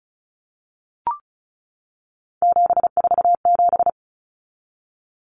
Morse code, heard as a keyed tone, about 34 wpm, 700 Hz. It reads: 747
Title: Numbers - 3-Digit 35wpm